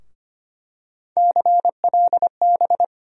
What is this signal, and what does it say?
Morse code, heard as a keyed tone, about 25 wpm, 700 Hz. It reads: CLB
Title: USA Call Sign Postfixes - 3 Letters 25wpm